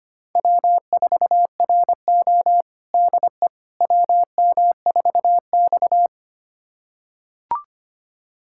Morse code, heard as a keyed tone, about 25 wpm, 700 Hz. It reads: W4RO DE WM4X
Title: USA Callsign DE Callsign - 4 Characters - 25wpm